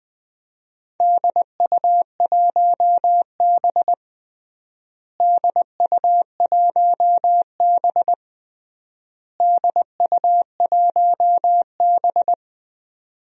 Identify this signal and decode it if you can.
Morse code, heard as a keyed tone, about 20 wpm, 700 Hz. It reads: DU1B DU1B DU1B
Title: DXCC-100 Call Signs - 4 Characters - Repeated 3x - 20wpm